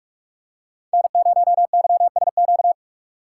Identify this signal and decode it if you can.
Morse code, heard as a keyed tone, about 42 wpm, 700 Hz. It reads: N0YSQ
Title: K1USN SST QSO Practice - 45wpm